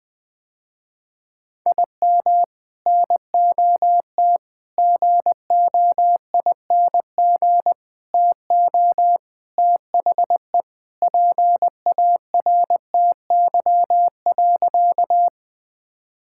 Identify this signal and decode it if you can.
Morse code, heard as a keyed tone, about 20 wpm, 700 Hz. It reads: IM NOT GOING TO THE PARTY.
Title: Sentences with 3-Word Phrases - Spoken Beginning - 20wpm